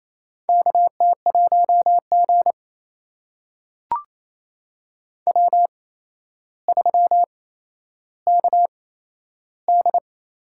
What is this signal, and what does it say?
Morse code, heard as a keyed tone, about 28 wpm, spaced as for 12 wpm, 700 Hz. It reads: KT1G W3KD
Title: USA Call Signs - 4 Characters - 8x Character Spacing - 28wpm